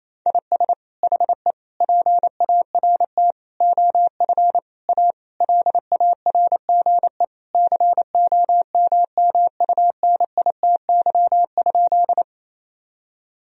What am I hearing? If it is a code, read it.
Morse code, heard as a keyed tone, about 28 wpm, 700 Hz. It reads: IS HE PART OF A LARGE COMMUNITY?
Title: Sentences from Top 1000 Words - Review Part 4 - 28wpm